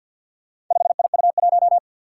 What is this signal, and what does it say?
Morse code, heard as a keyed tone, about 48 wpm, 700 Hz. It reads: 5IU1
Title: Four Challenging Letter-Number Combinations 50wpm